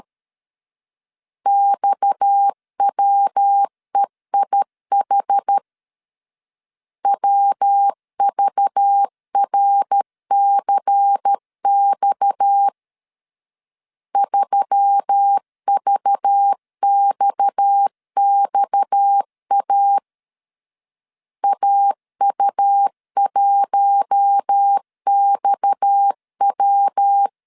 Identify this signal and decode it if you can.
Morse code, heard as a keyed tone, about 13 wpm, 800 Hz. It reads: XWEIH WVRCX 3VXXA AU1XW
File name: LEZIONI / 6 PKX0 / 6 PKX 12wpm 2024